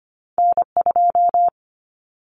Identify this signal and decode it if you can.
Morse code, heard as a keyed tone, about 25 wpm, 700 Hz. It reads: N2
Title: Common Cut Numbers - 2-Digit 25wpm